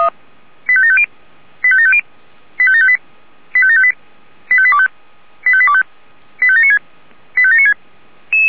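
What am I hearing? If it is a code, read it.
Five-tone selective call (ZVEI), heard as quick runs of five tones, tones of about 80 ms. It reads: 85750 85750 85758 85758 85825 85825 85796 85796
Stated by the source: ZVEI five-tone selective call received on an SDR.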